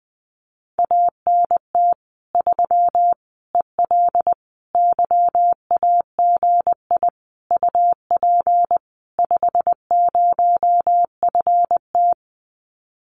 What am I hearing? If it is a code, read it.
Morse code, heard as a keyed tone, about 20 wpm, 700 Hz. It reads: ANT 3 EL YAGI UP 50FT
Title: Common QSO Phrases Encore 20wpm